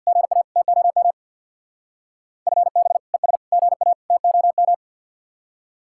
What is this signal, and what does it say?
Morse code, heard as a keyed tone, about 63 wpm, 700 Hz. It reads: GA TOM FB ES GA TOM